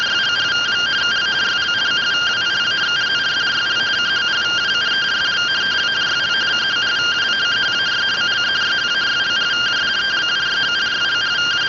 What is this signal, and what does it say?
Two-tone teleprinter signal, about 170 Hz shift.